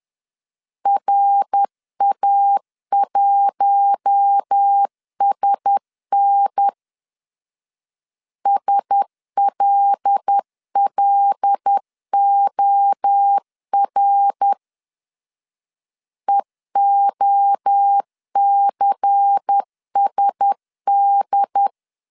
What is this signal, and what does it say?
Morse code, heard as a keyed tone, about 10 wpm, 800 Hz. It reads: RA1SN SLLOR EOCSD